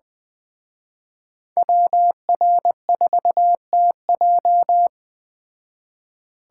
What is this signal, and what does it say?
Morse code, heard as a keyed tone, about 20 wpm, 700 Hz. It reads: WR4TJ